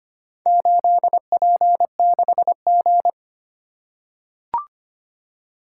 Morse code, heard as a keyed tone, about 25 wpm, 700 Hz. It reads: 8P6G
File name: practice / 25wpm / Four Letter-Number Combinations - 25wpm